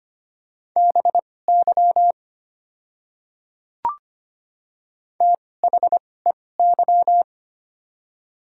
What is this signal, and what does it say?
Morse code, heard as a keyed tone, about 25 wpm, spaced as for 18 wpm, 700 Hz. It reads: BY THEY
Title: 2x Character Spacing - Top 100 Words - 25wpm